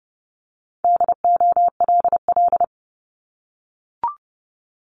Morse code, heard as a keyed tone, about 30 wpm, 700 Hz. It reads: DOLL
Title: Multi-letter Phonemes Repeated 3x 30wpm